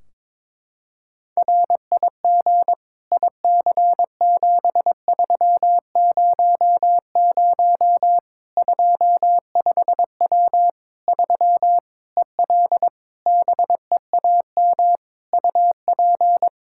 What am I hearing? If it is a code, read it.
Morse code, heard as a keyed tone, about 22 wpm, 700 Hz. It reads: RIG IC7300 25W 3 EL BEAM UP
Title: Practice QSO - Exchange 2 - Info - 22wpm